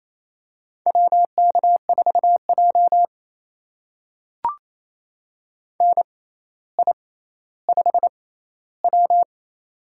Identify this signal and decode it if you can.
Morse code, heard as a keyed tone, about 28 wpm, spaced as for 14 wpm, 700 Hz. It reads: WK4J NI5W
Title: USA Call Signs - 6x Character Spacing - 28wpm